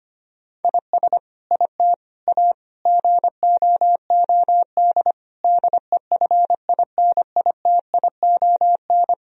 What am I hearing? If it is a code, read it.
Morse code, heard as a keyed tone, about 25 wpm, 700 Hz. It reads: IS IT A GOOD DEFINITION
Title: Sentences from Top 1000 Words - Review Part 4 - 25wpm